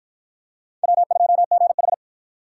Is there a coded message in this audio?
Yes: Morse code from 0.8 to 1.9 s, heard as a keyed tone, about 52 wpm, 700 Hz. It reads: W2GH